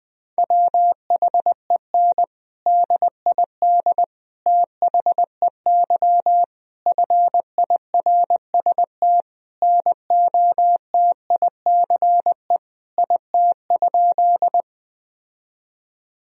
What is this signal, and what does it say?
Morse code, heard as a keyed tone, about 20 wpm, 700 Hz. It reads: WHEN DID THEY FIRST NOTICE IT?
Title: Sentences from Top 1000 Words - Review Part 2 - 20wpm